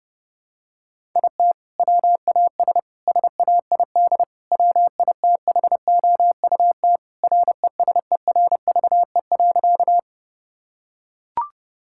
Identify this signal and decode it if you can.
Morse code, heard as a keyed tone, about 30 wpm, 700 Hz. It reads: IT WAS SAID WITHOUT RESERVE.
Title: Sentences from Top 1000 Words - Review Part 6 - 30wpm